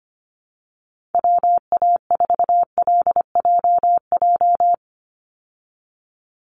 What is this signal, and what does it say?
Morse code, heard as a keyed tone, about 25 wpm, 700 Hz. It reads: WA4LJJ
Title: Call Signs Repeated 3x - 25wpm